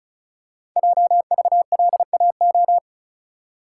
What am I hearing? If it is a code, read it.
Morse code, heard as a keyed tone, about 36 wpm, 690 Hz. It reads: JVLAO
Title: Code Groups - 5 Letter:Numbers - 35wpm